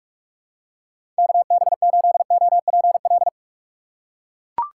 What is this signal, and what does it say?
Morse code, heard as a keyed tone, about 41 wpm, 700 Hz. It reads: KB8OPL
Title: USA Call Signs - 6 Characters - 6x Character Spacing - 45wpm